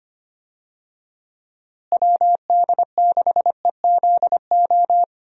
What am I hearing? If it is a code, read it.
Morse code, heard as a keyed tone, about 25 wpm, 700 Hz. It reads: WD6EZO
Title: USA Call Signs - 6 Characters - Repeated 3x - 25wpm